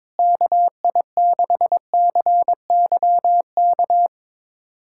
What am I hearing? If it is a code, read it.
Morse code, heard as a keyed tone, about 22 wpm, 700 Hz. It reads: KI6CYK
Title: USA Callsign DE Callsign - 6 Characters - 22wpm